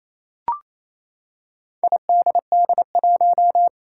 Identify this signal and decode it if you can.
Morse code, heard as a keyed tone, about 27 wpm, 700 Hz. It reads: IDD1
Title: Four Challenging Letter-Number Combinations 28wpm